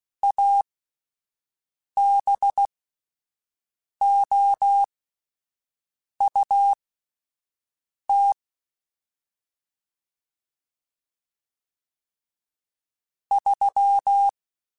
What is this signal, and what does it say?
Morse code, heard as a keyed tone, about 16 wpm, spaced as for 5 wpm, 800 Hz. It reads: ABOUT 3
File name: (spaced) / QSOs Spaced-mp3s / WD0BGP 1 @10 sp